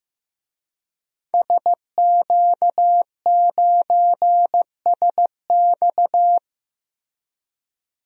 Morse code, heard as a keyed tone, about 15 wpm, 700 Hz. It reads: SQ9SX
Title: DXCC-100 Call Signs - 5 Characters - Repeated 3x - 15wpm